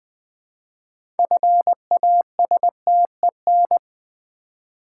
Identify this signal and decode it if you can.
Morse code, heard as a keyed tone, about 20 wpm, 700 Hz. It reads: FASTEN
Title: 6 Letter Words - First 3 Letters Given - 20wpm